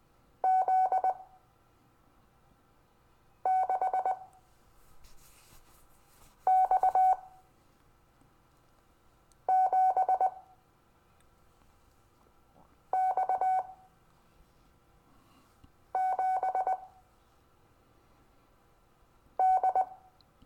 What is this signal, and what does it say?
Morse code, heard as a keyed tone, about 20 wpm, spaced as for 3 wpm, 750 Hz. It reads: Z6X7X7D
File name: cw / Lesson25 6 X